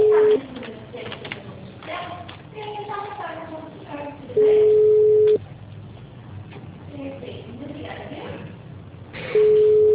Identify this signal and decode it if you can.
Ringback tone (Europe, 425 Hz), with speech.